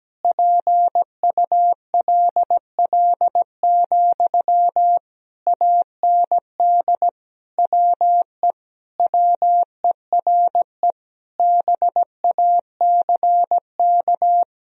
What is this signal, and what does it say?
Morse code, heard as a keyed tone, about 17 wpm, 700 Hz. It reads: PULL, AND WE WERE BACK
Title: Sentences from Top 1000 Words - Review Part 2 - 17wpm